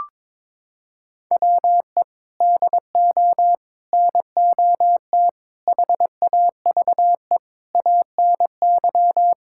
Morse code, heard as a keyed tone, about 22 wpm, 700 Hz. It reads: WE DO NOT HAVE ANY
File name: practice / 22wpm / Sentences with 4-Word Phrases - 22wpm